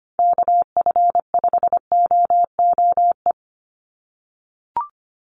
Morse code, heard as a keyed tone, about 25 wpm, 700 Hz. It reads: KF5OOE
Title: USA Call Signs 25wpm